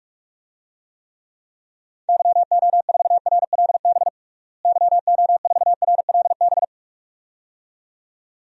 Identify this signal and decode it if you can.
Morse code, heard as a keyed tone, about 46 wpm, 700 Hz. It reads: YO4RLB YO4RLB
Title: DXCC-100 Call Signs - 6 Characters - Repeated 2x - 45wpm